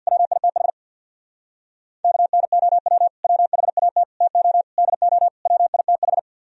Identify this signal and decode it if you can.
Morse code, heard as a keyed tone, about 63 wpm, 700 Hz. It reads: WITH KNOW WHAT TO DO WITH